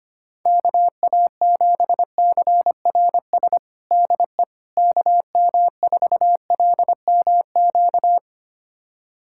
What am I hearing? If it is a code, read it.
Morse code, heard as a keyed tone, about 25 wpm, 700 Hz. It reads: KA7CRS DE KM4LMQ
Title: USA Callsign DE Callsign 25wpm